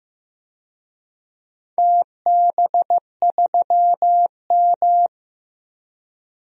Morse code, heard as a keyed tone, about 15 wpm, 700 Hz. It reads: TB3M
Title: DXCC-100 Call Signs - 4 Characters - 15wpm